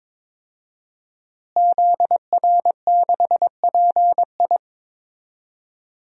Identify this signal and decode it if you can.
Morse code, heard as a keyed tone, about 22 wpm, 700 Hz. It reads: ZR6PI
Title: DXCC-100 Call Signs - 5 Characters - Repeated 2x - 22wpm